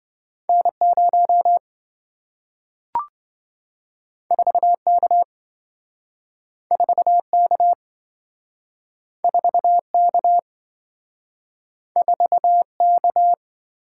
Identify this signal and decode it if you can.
Morse code, heard as a keyed tone, about 27 wpm, 700 Hz. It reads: N0 4K 4K 4K 4K